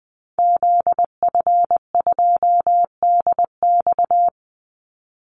Morse code, heard as a keyed tone, about 20 wpm, 700 Hz. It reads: ZF2DX
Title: Call Signs - 20wpm